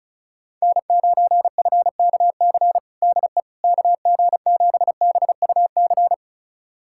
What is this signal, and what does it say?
Morse code, heard as a keyed tone, about 36 wpm, 700 Hz. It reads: N9FKC DE KG7BUC
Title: USA Callsign DE Callsign 35wpm